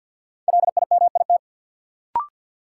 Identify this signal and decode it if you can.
Morse code, heard as a keyed tone, about 53 wpm, 700 Hz. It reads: LIMIT
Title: Sentences from Top 1500 Words - New Word Drill - 50wpm